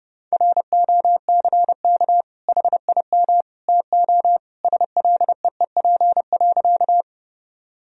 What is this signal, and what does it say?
Morse code, heard as a keyed tone, about 32 wpm, 700 Hz. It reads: ROCK HIM TO SLEEP.
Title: Sentences from Top 1000 Words - Review Part 2 - 30wpm